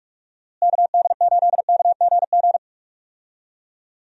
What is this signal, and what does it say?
Morse code, heard as a keyed tone, about 46 wpm, 700 Hz. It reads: KD8KGG
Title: USA Call Signs - 6 Characters - Repeated 3x - 45wpm